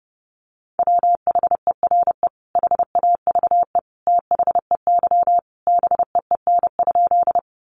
Morse code, heard as a keyed tone, about 30 wpm, 700 Hz. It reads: WHERE HAVE THEY BEEN?
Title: Sentences from Top 200 Words 30wpm